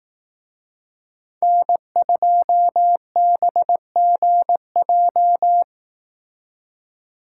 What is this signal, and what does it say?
Morse code, heard as a keyed tone, about 18 wpm, 700 Hz. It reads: N2BGJ
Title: Speed Racing - USA 1x3 Call Signs 15wpm